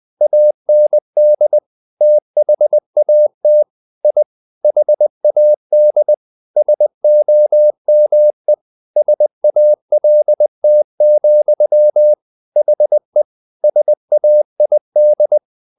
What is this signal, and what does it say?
Morse code, heard as a keyed tone, about 20 wpm, 600 Hz. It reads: AND THAT I HAD SOME SALT, HE SAID
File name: the old man and the sea / the old man and the sea part 1 20wpm